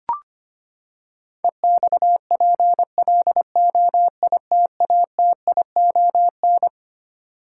Morse code, heard as a keyed tone, about 25 wpm, 700 Hz. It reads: EXPLOITATION
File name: Twelve Letter Words - 25wpm